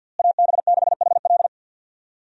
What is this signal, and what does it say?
Morse code, heard as a keyed tone, about 47 wpm, 700 Hz. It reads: AB6HL
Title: USA Call Signs - 5 Characters - Repeated 3x - 50wpm